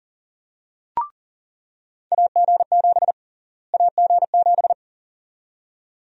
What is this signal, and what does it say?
Morse code, heard as a keyed tone, about 40 wpm, 700 Hz. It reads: AG7 AG7
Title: USA Call Sign Prefixes - Repeated 2x - 40wpm